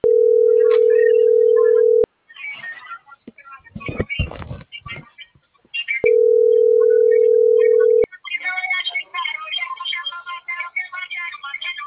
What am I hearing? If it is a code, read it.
Ringback tone (North America), 440 + 480 Hz.